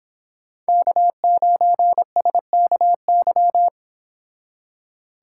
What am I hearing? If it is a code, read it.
Morse code, heard as a keyed tone, about 26 wpm, 700 Hz. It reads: K9SKY
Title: Speed Racing - USA 1x3 Call Signs 22wpm